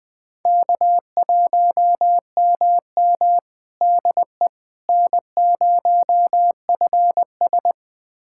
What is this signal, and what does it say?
Morse code, heard as a keyed tone, about 21 wpm, 700 Hz. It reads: K1MM DE N0FS
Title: USA Callsign DE Callsign - 4 Characters - 20wpm